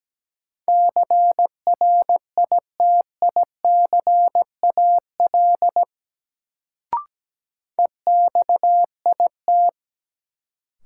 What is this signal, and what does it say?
Morse code, heard as a keyed tone, about 17 wpm, 700 Hz. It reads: CRITICAL EXIT
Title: Sentences from Top 1100 Words - New Word Drill Alt - 17wpm